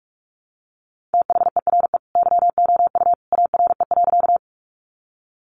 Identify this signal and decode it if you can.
Morse code, heard as a keyed tone, about 44 wpm, 700 Hz. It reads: THERE YOU ARE.
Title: Sentences from Top 100 Words 45wpm